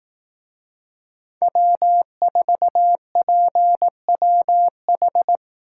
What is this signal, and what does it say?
Morse code, heard as a keyed tone, about 19 wpm, 700 Hz. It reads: W4PWH
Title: Speed Racing - USA 1x3 Call Signs 15wpm